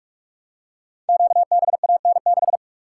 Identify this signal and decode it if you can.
Morse code, heard as a keyed tone, about 44 wpm, 700 Hz. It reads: QBAN6
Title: Code Groups - 5 Letter:Numbers - 45wpm